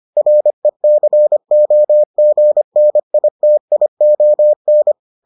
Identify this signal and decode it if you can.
Morse code, heard as a keyed tone, about 25 wpm, 600 Hz. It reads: RECOGNITION